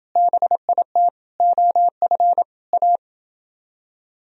Morse code, heard as a keyed tone, about 28 wpm, 700 Hz. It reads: BIT OF A